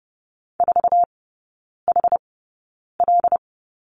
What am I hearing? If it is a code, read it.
Morse code, heard as a keyed tone, about 30 wpm, spaced as for 9 wpm, 700 Hz. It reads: 4HL
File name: Lesson 17e - Letter u - Three Characters 30wpm